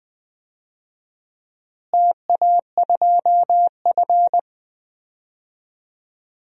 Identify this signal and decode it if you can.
Morse code, heard as a keyed tone, about 21 wpm, 700 Hz. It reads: TA2F
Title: DXCC-100 Call Signs - 4 Characters - Repeated 2x - 20wpm